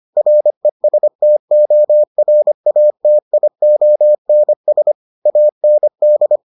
Morse code, heard as a keyed tone, about 25 wpm, 600 Hz. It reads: RESTORATIONS AND